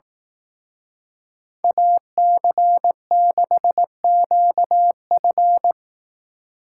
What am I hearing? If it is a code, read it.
Morse code, heard as a keyed tone, about 18 wpm, 700 Hz. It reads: AC6QF
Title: Speed Racing - USA 2x2 Call Signs 15wpm